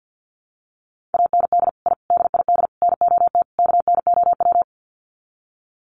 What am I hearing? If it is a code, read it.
Morse code, heard as a keyed tone, about 50 wpm, 700 Hz. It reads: AND I DID NOT KNOW
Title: Sets of 5 Words Encore 50wpm